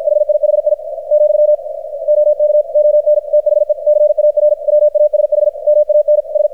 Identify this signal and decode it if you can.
Morse code, heard as a keyed tone, about 51 wpm, 600 Hz, with radio noise. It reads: HIHI , OK ON THE CAR WASH MAN S
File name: archived / CRSnet